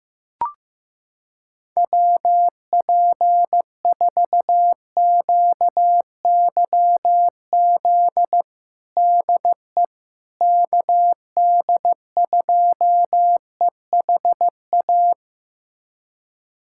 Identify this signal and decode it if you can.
Morse code, heard as a keyed tone, about 15 wpm, 700 Hz. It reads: WP4QYZ DE KD2EHA
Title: USA Callsign DE Callsign - 6 Characters - 15wpm